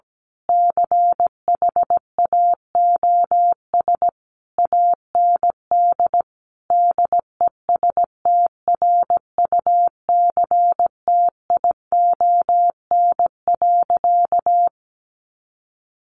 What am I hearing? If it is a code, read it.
Morse code, heard as a keyed tone, about 17 wpm, 700 Hz. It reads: CHAOS AND DESTRUCTION.
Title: A Sound Of Thunder 17wpm